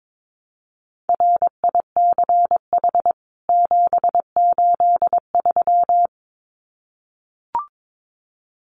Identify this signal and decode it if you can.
Morse code, heard as a keyed tone, about 22 wpm, 700 Hz. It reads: RICH 783
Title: CWT Exchange 22wpm